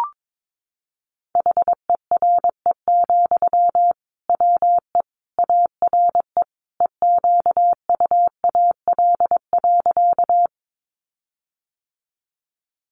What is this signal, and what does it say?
Morse code, heard as a keyed tone, about 22 wpm, 700 Hz. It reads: HERE, WE ARE EQUAL.
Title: Sentences from Top 1400 Words 22wpm